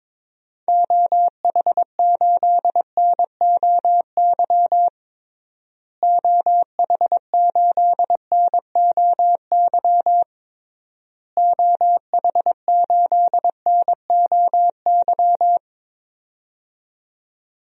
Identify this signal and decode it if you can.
Morse code, heard as a keyed tone, about 22 wpm, 700 Hz. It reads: OH8NOY OH8NOY OH8NOY
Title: DXCC-100 Call Signs - 6 Characters - Repeated 3x - 22wpm